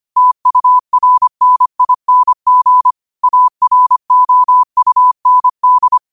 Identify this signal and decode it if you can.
Morse code, heard as a keyed tone, about 25 wpm, 1000 Hz. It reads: TURNING AROUND
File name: qotdfiles / 25wpm